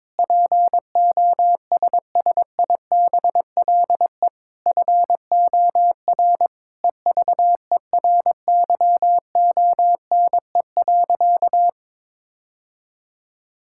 Morse code, heard as a keyed tone, about 22 wpm, 700 Hz. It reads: POSSIBLE FOR EVERYONE.
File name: practice / 22wpm / Sentences with 8-Word Phrases - 22wpm